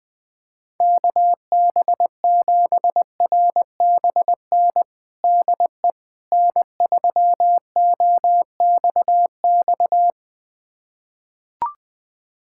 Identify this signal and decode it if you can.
Morse code, heard as a keyed tone, about 21 wpm, 700 Hz. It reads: KB7RBN DE N3OXX
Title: USA Callsign DE Callsign 20wpm